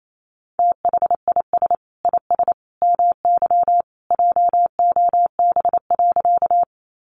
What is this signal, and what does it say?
Morse code, heard as a keyed tone, about 28 wpm, 700 Hz. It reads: THIS IS MY JOB.
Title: Sentences from Top 600 Words 28wpm